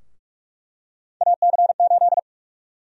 Morse code, heard as a keyed tone, about 46 wpm, 700 Hz. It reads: AC8
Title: USA Call Sign Prefixes - 45wpm